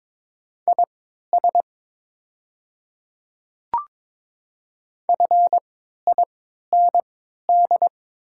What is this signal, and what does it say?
Morse code, heard as a keyed tone, about 22 wpm, spaced as for 12 wpm, 700 Hz. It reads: IS FIND
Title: 3x Character Spacing - Top 100 Words - 22wpm